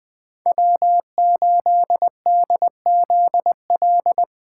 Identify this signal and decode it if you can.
Morse code, heard as a keyed tone, about 19 wpm, 700 Hz. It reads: W8DZL
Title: Speed Racing - USA 1x3 Call Signs 17wpm